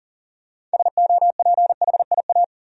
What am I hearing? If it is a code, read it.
Morse code, heard as a keyed tone, about 40 wpm, 700 Hz. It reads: SOPHIA
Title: USA Common Names - 40wpm